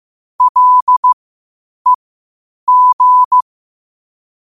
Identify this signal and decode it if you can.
Morse code, heard as a keyed tone, about 15 wpm, spaced as for 8 wpm, 1000 Hz. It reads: LEG
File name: 8wpm